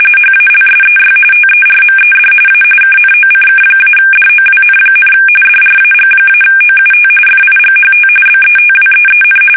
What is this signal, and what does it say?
RTTY, about 45.45 baud. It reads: HALSEY TERMED A KNOCK-DOWN, DRAG-OUT FIGHT BETWEEN
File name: fsk850 / battleshipnj